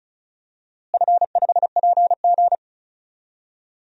Morse code, heard as a keyed tone, about 36 wpm, 700 Hz. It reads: F5PG
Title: DXCC-100 Call Signs - 4 Characters - Repeated 3x - 35wpm